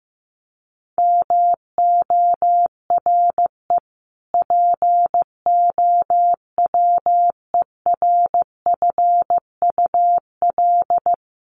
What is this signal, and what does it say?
Morse code, heard as a keyed tone, about 15 wpm, 700 Hz. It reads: MORE POWERFUL
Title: Sentences from Top 1300 Words 15wpm